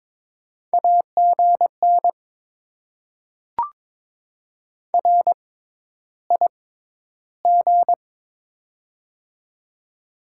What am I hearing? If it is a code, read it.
Morse code, heard as a keyed tone, about 22 wpm, spaced as for 11 wpm, 700 Hz. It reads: AGN RIG
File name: Top 20 QSO Elements - 6x Character Spacing - 22wpm